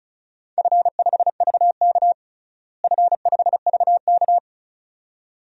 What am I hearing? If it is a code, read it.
Morse code, heard as a keyed tone, about 36 wpm, 700 Hz. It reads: F5VK F5VK
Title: DXCC-100 Call Signs - 4 Characters - Repeated 2x - 35wpm